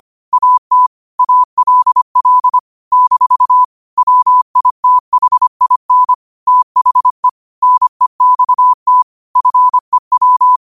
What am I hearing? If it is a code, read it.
Morse code, heard as a keyed tone, about 25 wpm, 1000 Hz. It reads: AT ALL - WITHIN THE NEXT FEW
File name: qotdfiles / 25wpm